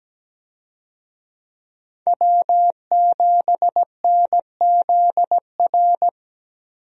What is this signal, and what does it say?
Morse code, heard as a keyed tone, about 17 wpm, 700 Hz. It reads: W7NZR
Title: USA Call Signs - 5 Characters - 17wpm